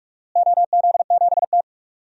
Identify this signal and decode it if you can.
Morse code, heard as a keyed tone, about 46 wpm, 690 Hz. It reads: OZ7T
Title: DXCC-100 Call Signs - 4 Characters - Repeated 3x - 45wpm